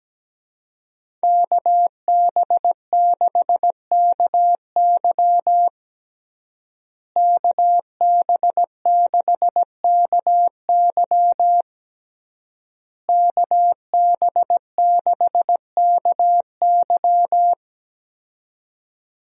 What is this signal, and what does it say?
Morse code, heard as a keyed tone, about 17 wpm, 700 Hz. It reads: KB6KY KB6KY KB6KY
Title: USA Call Signs - 5 Characters - Repeated 3x - 17wpm